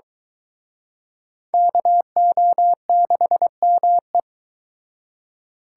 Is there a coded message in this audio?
Yes: Morse code from 1.5 to 4.2 s, heard as a keyed tone, about 23 wpm, 700 Hz. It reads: KO6ME